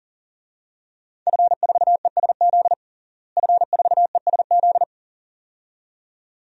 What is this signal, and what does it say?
Morse code, heard as a keyed tone, about 37 wpm, 700 Hz. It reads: F4ESZ F4ESZ
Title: DXCC-100 Call Signs - 5 Characters - Repeated 2x - 40wpm